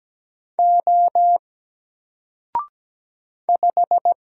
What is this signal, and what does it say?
Morse code, heard as a keyed tone, about 17 wpm, 700 Hz. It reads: O 5
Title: Single Letter-Number Alt 17wpm